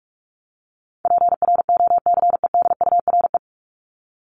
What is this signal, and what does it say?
Morse code, heard as a keyed tone, about 44 wpm, 700 Hz. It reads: PROCEDURE
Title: Top 1751-2000 Words 45wpm